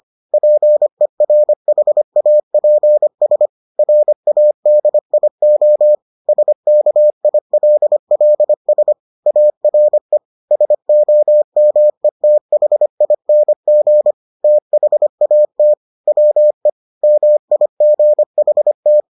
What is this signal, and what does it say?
Morse code, heard as a keyed tone, about 25 wpm, 600 Hz. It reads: PERHAPS RADIO SKILLS ARE SOMETHING THAT WE MIGHT